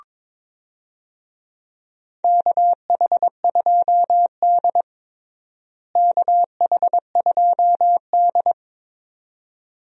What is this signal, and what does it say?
Morse code, heard as a keyed tone, about 22 wpm, 700 Hz. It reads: KH2D KH2D
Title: USA Call Signs - 4 Characters - Repeated 2x - 22wpm